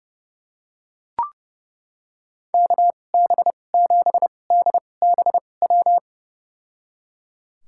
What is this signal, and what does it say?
Morse code, heard as a keyed tone, about 31 wpm, spaced as for 22 wpm, 700 Hz. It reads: KB7DBW
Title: USA Call Signs - 2x Character Spacing - 30wpm